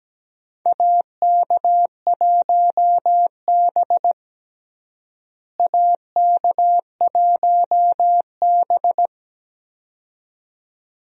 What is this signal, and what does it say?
Morse code, heard as a keyed tone, about 17 wpm, 700 Hz. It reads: AK1B AK1B
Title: USA Call Signs - 4 Characters - Repeated 2x - 17wpm